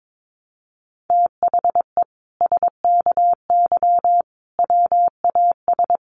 Morse code, heard as a keyed tone, about 22 wpm, 700 Hz. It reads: THE SKY WAS
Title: Sentences from Top 1200 Words 22wpm